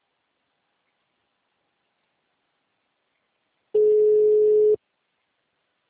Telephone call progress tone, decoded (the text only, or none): ringback tone